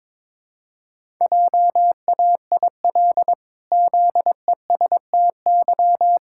Morse code, heard as a keyed tone, about 22 wpm, 700 Hz. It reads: JAIL ZESTY
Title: Pangram Challenge v2 - 22wpm